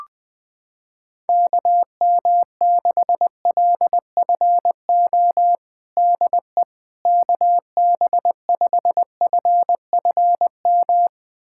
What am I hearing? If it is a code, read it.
Morse code, heard as a keyed tone, about 20 wpm, 700 Hz. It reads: KM6LFO DE KB5FFM
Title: USA Callsign DE Callsign - 6 Characters - 20wpm